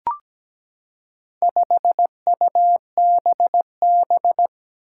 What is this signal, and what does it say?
Morse code, heard as a keyed tone, about 17 wpm, 700 Hz. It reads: 5UBB
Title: Four Challenging Letter-Number Combinations 17wpm